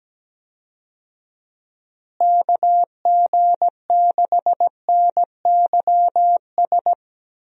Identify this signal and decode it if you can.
Morse code, heard as a keyed tone, about 17 wpm, 700 Hz. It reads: KG6NYS
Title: USA Call Signs - 6 Characters - Repeated 3x - 17wpm